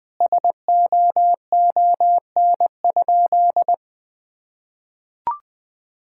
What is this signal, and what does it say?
Morse code, heard as a keyed tone, about 21 wpm, 700 Hz. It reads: SOON?
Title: Sentences from Top 1600 Words 20wpm